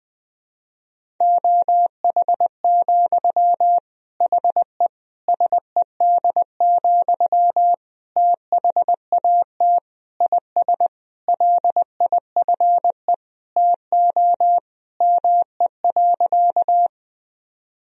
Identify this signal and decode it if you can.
Morse code, heard as a keyed tone, about 20 wpm, 700 Hz. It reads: OH, HE SED, THAT IS LIFE TO ME.
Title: Sentences from Top 500 Words and Abbreviations 20wpm